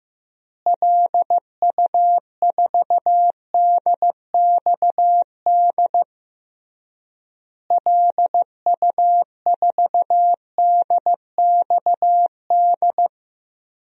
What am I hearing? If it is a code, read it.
Morse code, heard as a keyed tone, about 15 wpm, 700 Hz. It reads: LU4DXD LU4DXD
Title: DXCC-100 Call Signs - 6 Characters - Repeated 3x - 15wpm